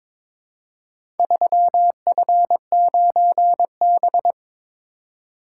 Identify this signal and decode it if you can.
Morse code, heard as a keyed tone, about 22 wpm, 700 Hz. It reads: 3F9B
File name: practice / 22wpm / Four Letter-Number Combinations - 22wpm